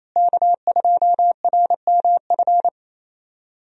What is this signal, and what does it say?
Morse code, heard as a keyed tone, about 28 wpm, 700 Hz. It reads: K2RMF
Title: USA Call Signs - 5 Characters - Repeated 2x - 28wpm